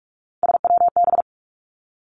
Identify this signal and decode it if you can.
Morse code, heard as a keyed tone, about 46 wpm, 700 Hz. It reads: SWB
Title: Three Challenging Letter-Number Combinations 45wpm